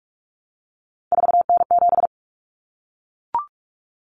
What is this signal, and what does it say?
Morse code, heard as a keyed tone, about 44 wpm, 700 Hz. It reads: 4N7
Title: Common Cut Numbers - 3-Digit 45wpm